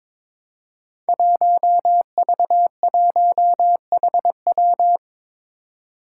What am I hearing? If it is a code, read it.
Morse code, heard as a keyed tone, about 23 wpm, 700 Hz. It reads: 1V1HW
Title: Five Challenging Letter-Number Combinations 22wpm